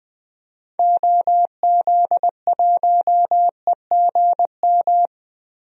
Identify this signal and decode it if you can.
Morse code, heard as a keyed tone, about 20 wpm, 700 Hz. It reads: OZ1EGM
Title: DXCC-100 Call Signs - 6 Characters - Repeated 3x - 20wpm